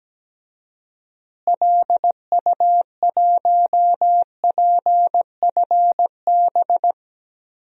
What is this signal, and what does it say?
Morse code, heard as a keyed tone, about 17 wpm, 700 Hz. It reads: LU1PFB
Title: DXCC-100 Call Signs - 6 Characters - Repeated 2x - 17wpm